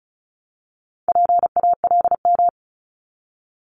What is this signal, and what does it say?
Morse code, heard as a keyed tone, about 36 wpm, 700 Hz. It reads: PALM
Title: Multi-letter Phonemes 35wpm